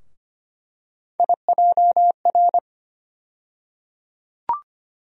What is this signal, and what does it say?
Morse code, heard as a keyed tone, about 25 wpm, 700 Hz. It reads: IJR
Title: USA Call Sign Postfixes - 3 Letters 25wpm